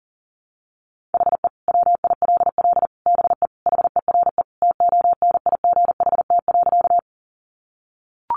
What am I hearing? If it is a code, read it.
Morse code, heard as a keyed tone, about 40 wpm, 700 Hz. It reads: HE WILL BE HERE TONIGHT.
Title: Sentences from Top 600 Words 40wpm